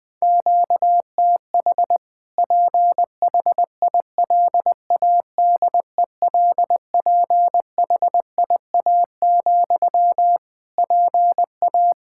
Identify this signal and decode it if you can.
Morse code, heard as a keyed tone, about 20 wpm, 700 Hz. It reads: QTH PHILADELPHIA, PA
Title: Practice QSO - Exchange 1 - Beginning - 20wpm